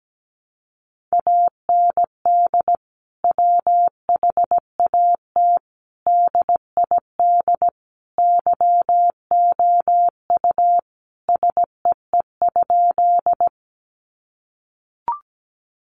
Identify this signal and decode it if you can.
Morse code, heard as a keyed tone, about 17 wpm, 700 Hz. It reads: AND WHAT DID YOU SEE?
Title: Sentences from Top 100 Words 17wpm